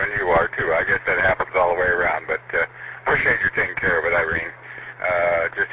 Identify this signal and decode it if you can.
Emergency Alert System (SAME) tones, heard as harsh buzzing bursts, with speech. It is heard from 0.8 to 4.6 s.